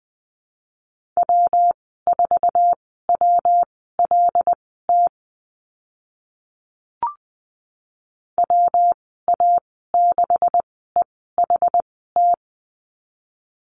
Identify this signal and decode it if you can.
Morse code, heard as a keyed tone, about 20 wpm, spaced as for 14 wpm, 700 Hz. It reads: W4WLT WA6EHT
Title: 140 Lesson 23g - Letter f - Call Signs 20wpm